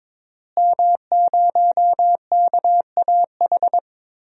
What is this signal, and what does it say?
Morse code, heard as a keyed tone, about 22 wpm, 700 Hz. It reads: M0KAH
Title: DXCC-100 Call Signs - 5 Characters - Repeated 3x - 22wpm